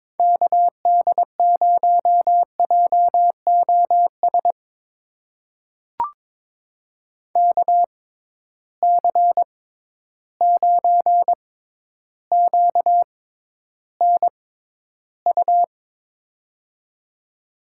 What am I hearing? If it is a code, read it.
Morse code, heard as a keyed tone, about 22 wpm, spaced as for 11 wpm, 700 Hz. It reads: KD0JOS KC9QNU
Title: USA Call Signs - 6 Characters - 6x Character Spacing - 22wpm